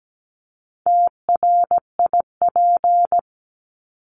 Morse code, heard as a keyed tone, about 17 wpm, 700 Hz. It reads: TRIP
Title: Top 1251-1500 Words 17wpm